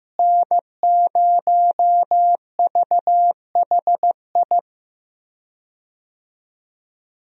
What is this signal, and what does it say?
Morse code, heard as a keyed tone, about 15 wpm, 700 Hz. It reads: N0VHI
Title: USA Call Signs - 5 Characters - Repeated 2x - 15wpm